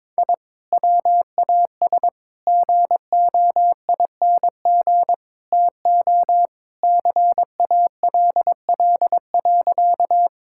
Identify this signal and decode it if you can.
Morse code, heard as a keyed tone, about 22 wpm, 700 Hz. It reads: I WAS GOING TO CALL.
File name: practice / 22wpm / Sentences with 4-Word Phrases - 22wpm